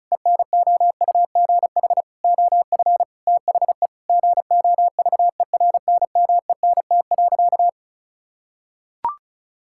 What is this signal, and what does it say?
Morse code, heard as a keyed tone, about 36 wpm, 700 Hz. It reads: ENOUGH OF THE GOVERNMENT.
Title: Sentences from Top 1600 Words 35wpm